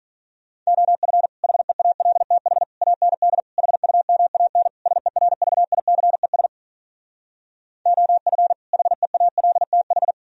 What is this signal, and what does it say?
Morse code, heard as a keyed tone, about 46 wpm, 700 Hz. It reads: OF HEALTH AND HUMAN SERVICES OF HEALTH